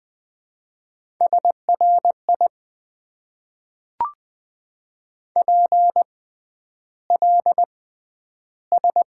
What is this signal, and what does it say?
Morse code, heard as a keyed tone, about 20 wpm, spaced as for 10 wpm, 700 Hz. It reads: SRI PLS
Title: Top 90 QSO Elements - 6x Character Spacing - 20wpm